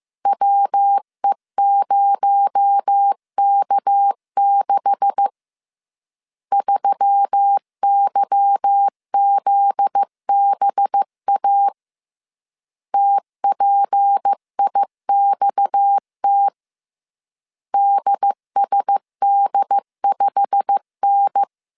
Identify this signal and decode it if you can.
Morse code, heard as a keyed tone, about 15 wpm, 800 Hz. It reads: WE0K6 3YZBA TPIXT DSD5N
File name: CW / DET / 1 9 1500C S C 14wpm 3245